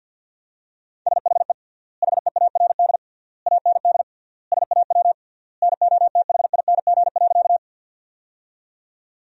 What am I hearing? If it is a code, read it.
Morse code, heard as a keyed tone, about 47 wpm, 700 Hz. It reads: SHE HEARD AND SAW NOTHING.
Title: 3x Word-Spacing - Sentences from Top 500 Words - 50wpm